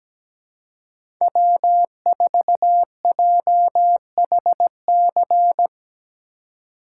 Morse code, heard as a keyed tone, about 17 wpm, 700 Hz. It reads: W4JHC
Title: USA Call Signs - 5 Characters - 2x Character Spacing - 17wpm